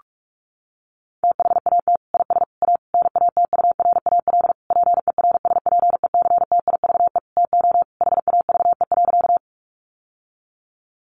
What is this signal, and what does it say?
Morse code, heard as a keyed tone, about 44 wpm, 700 Hz. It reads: THAT IS A NATURAL PERSPECTIVE TO HAVE.
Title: Sentences from Top 1200 Words 45wpm